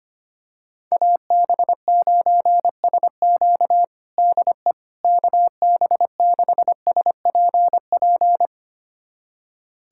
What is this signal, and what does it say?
Morse code, heard as a keyed tone, about 25 wpm, 700 Hz. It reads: AB9SQ DE KB6SPP
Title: USA Callsign DE Callsign 25wpm